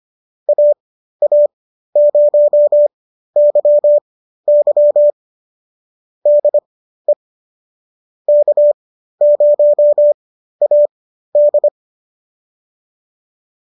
Morse code, heard as a keyed tone, about 25 wpm, spaced as for 13 wpm, 570 Hz. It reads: AA0YY DE K0AD